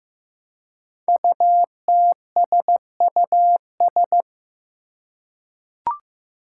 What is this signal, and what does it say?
Morse code, heard as a keyed tone, about 15 wpm, 700 Hz. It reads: UTSUS
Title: Five Challenging Letter-Number Combinations 15wpm